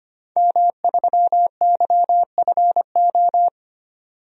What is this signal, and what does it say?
Morse code, heard as a keyed tone, about 25 wpm, 700 Hz. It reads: M3YFO
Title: DXCC-100 Call Signs - 5 Characters - 25wpm